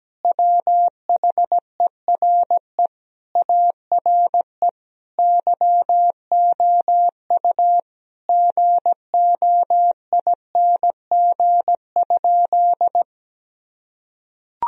Morse code, heard as a keyed tone, about 17 wpm, 700 Hz. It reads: WHERE ARE YOU GOING?
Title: Sentences from Top 1000 Words - Review Part 2 - 17wpm